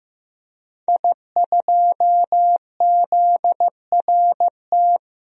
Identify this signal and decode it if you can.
Morse code, heard as a keyed tone, about 15 wpm, 700 Hz. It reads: I2ZRT
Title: DXCC-100 Call Signs - 5 Characters - Repeated 3x - 15wpm